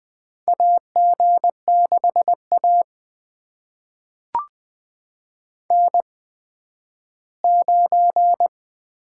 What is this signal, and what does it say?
Morse code, heard as a keyed tone, about 20 wpm, spaced as for 12 wpm, 700 Hz. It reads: AG6A N9
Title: USA Call Signs - 8x Character Spacing - 20wpm